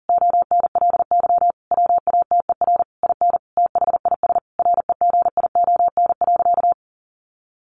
Morse code, heard as a keyed tone, about 40 wpm, 700 Hz. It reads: ONLY WATER IN THIS REGION.
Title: Sentences from Top 1500 Words 40wpm